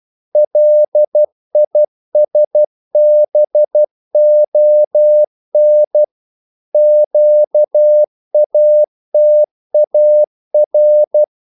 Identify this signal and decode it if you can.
Morse code, heard as a keyed tone, about 12 wpm, 600 Hz. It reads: LISBON QATAR
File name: CW / Practice / NationalCapitals12